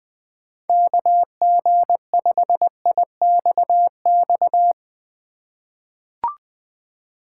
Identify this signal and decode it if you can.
Morse code, heard as a keyed tone, about 20 wpm, 700 Hz. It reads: KG5IXX
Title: USA Call Signs - 6 Characters - 2x Character Spacing - 20wpm